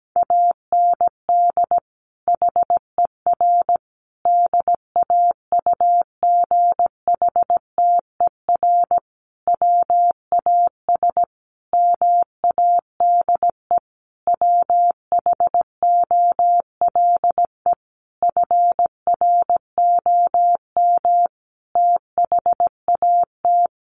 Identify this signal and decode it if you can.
Morse code, heard as a keyed tone, about 17 wpm, 700 Hz. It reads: AND HER DAUGHTER WAS MADE WHOLE FROM THAT
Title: Sentences from Top 1500 Words 17wpm